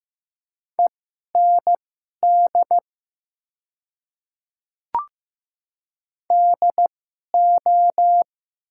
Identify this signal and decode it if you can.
Morse code, heard as a keyed tone, about 15 wpm, spaced as for 11 wpm, 700 Hz. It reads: END DO
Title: 2x Character Spacing - Top 200 Words - 15wpm